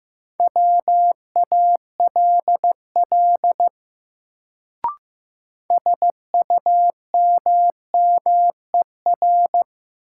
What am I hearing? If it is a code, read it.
Morse code, heard as a keyed tone, about 15 wpm, 700 Hz. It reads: WALL SUMMER
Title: Sentences from Top 1000 Words - Review Part 6 - New Word Drill - 15wpm